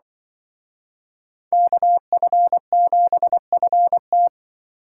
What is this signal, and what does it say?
Morse code, heard as a keyed tone, about 24 wpm, 700 Hz. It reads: KF7FT
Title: Speed Racing - USA 2x2 Call Signs 20wpm